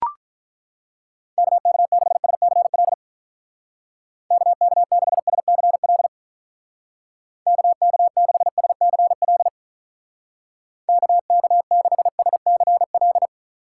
Morse code, heard as a keyed tone, about 42 wpm, 700 Hz. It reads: KK6SCL KK6SCL KK6SCL KK6SCL
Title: Speed Racing - USA 2x3 Call Signs 35wpm